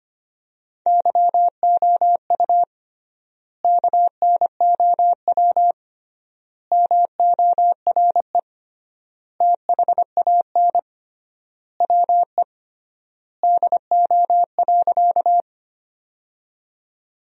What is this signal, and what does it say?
Morse code, heard as a keyed tone, about 25 wpm, 700 Hz. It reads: YOU KNOW MORE THAN WE DO.
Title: 3x Word-Spacing - Sentences from Top 100 Words - 25wpm